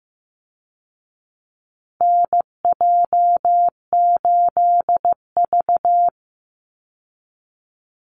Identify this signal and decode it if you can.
Morse code, heard as a keyed tone, about 15 wpm, 700 Hz. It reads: NJ8V
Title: Call Signs Repeated 2x - 15wpm